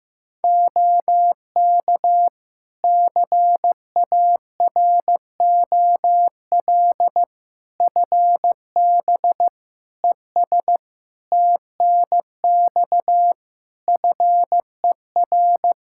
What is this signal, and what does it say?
Morse code, heard as a keyed tone, about 15 wpm, 700 Hz. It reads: OK CAROL FB ES TNX FER
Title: K1USN SST QSO Practice 15wpm